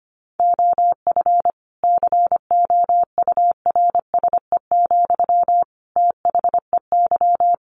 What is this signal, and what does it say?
Morse code, heard as a keyed tone, about 25 wpm, 700 Hz. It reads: OF COURSE, THEY
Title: Sentences from Top 1500 Words 25wpm